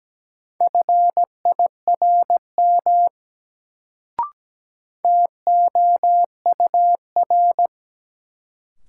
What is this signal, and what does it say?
Morse code, heard as a keyed tone, about 17 wpm, 700 Hz. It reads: FIRM TOUR
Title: Sentences from Top 1400 Words - New Word Drill - 17wpm